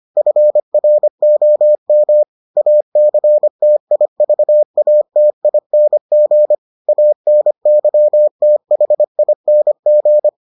Morse code, heard as a keyed tone, about 25 wpm, 600 Hz. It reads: FROM ACTIVATING ANYTHING